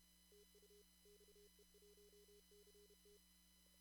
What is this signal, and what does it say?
Morse code, heard as a keyed tone, about 30 wpm, 410 Hz, with noise. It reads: TU XE1CT
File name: CW / kt5j wpx cw-op-k5pi